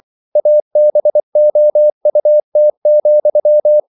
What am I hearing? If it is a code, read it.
Morse code, heard as a keyed tone, about 24 wpm, 600 Hz. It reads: ABOUT,
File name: CW / Practice / If24